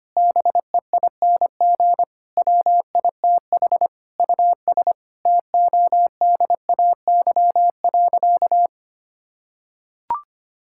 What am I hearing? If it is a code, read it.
Morse code, heard as a keyed tone, about 25 wpm, 700 Hz. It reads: BEING WITH US TODAY.
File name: practice / 25wpm / Sentences with 9-Word Phrases - 25wpm